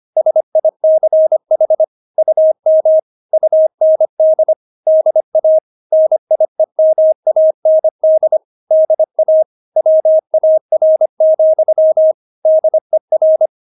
Morse code, heard as a keyed tone, about 25 wpm, 630 Hz. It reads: SICH UM UND DA NIEMAND DA WAR, DER
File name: Witz6 25wpm